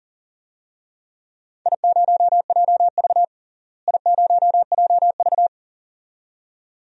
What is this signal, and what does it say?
Morse code, heard as a keyed tone, about 40 wpm, 700 Hz. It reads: I0JV I0JV
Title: DXCC-100 Call Signs - 4 Characters - Repeated 2x - 40wpm